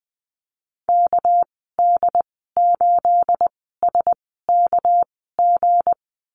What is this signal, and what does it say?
Morse code, heard as a keyed tone, about 20 wpm, spaced as for 15 wpm, 700 Hz. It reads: KD8SKG
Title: 231 Lesson 37g - Letter x - Call Signs 20wpm